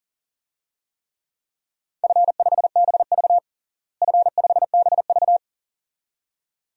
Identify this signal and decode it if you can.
Morse code, heard as a keyed tone, about 40 wpm, 700 Hz. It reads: F5BV F5BV
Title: DXCC-100 Call Signs - 4 Characters - Repeated 2x - 40wpm